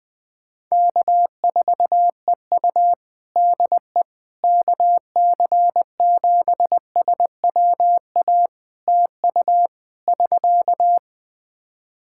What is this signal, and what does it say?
Morse code, heard as a keyed tone, about 20 wpm, 700 Hz. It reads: K4EU DE KC7SWA TU <SK>
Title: Practice QSO - Exchange 4 - Ending - 20wpm